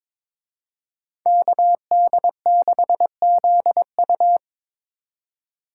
Morse code, heard as a keyed tone, about 26 wpm, 700 Hz. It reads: KD6ZU
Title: Speed Racing - USA 2x2 Call Signs 22wpm